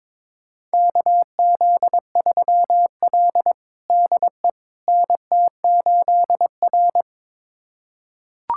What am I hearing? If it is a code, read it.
Morse code, heard as a keyed tone, about 22 wpm, 700 Hz. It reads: KZ3L DE NT8R
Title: USA Callsign DE Callsign - 4 Characters - 22wpm